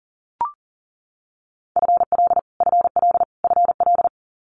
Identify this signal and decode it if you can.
Morse code, heard as a keyed tone, about 42 wpm, 700 Hz. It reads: FL FL FL
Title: Multi-letter Phonemes Repeated 3x 40wpm